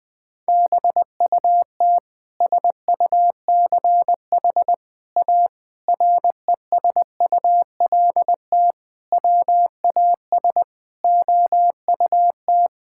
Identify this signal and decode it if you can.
Morse code, heard as a keyed tone, about 21 wpm, 700 Hz. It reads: BUT SUCH A RESULT WAS OUT
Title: Sentences from Top 1000 Words - Review Part 5 - 20wpm